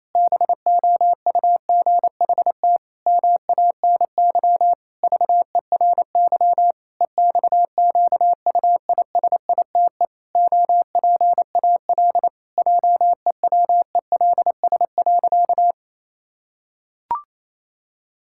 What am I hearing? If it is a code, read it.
Morse code, heard as a keyed tone, about 28 wpm, 700 Hz. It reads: BOUGHT MANY VERY EXQUISITE OPAL JEWELS.
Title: Pangram Challenge v2 - 28wpm